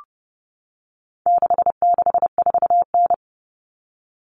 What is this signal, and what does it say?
Morse code, heard as a keyed tone, about 30 wpm, 700 Hz. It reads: 664N
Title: Common Cut Numbers - 4-Digit 30wpm